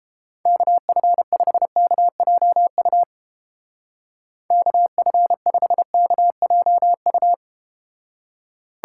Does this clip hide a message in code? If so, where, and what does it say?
Yes: Morse code from 0.5 to 7.3 s, heard as a keyed tone, about 31 wpm, 700 Hz. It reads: KF5KJU KF5KJU